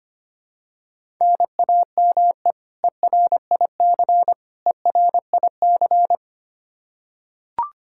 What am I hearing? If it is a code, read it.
Morse code, heard as a keyed tone, about 25 wpm, 700 Hz. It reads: NAME ERIC ERIC
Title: Practice QSO - Exchange 1 - Beginning - 25wpm